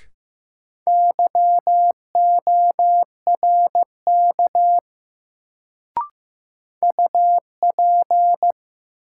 Morse code, heard as a keyed tone, about 15 wpm, 700 Hz. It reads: YORK UP
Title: Sets of 2 Words - New Word Drill - 15wpm